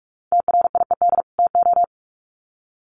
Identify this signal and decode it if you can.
Morse code, heard as a keyed tone, about 44 wpm, 700 Hz. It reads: TRIED TO
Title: Sets of 2 Words 45wpm